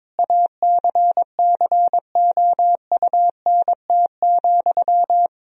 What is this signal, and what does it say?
Morse code, heard as a keyed tone, about 22 wpm, 700 Hz. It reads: ACCOUNT,
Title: Sentences from Top 1000 Words - Review Part 4 - 22wpm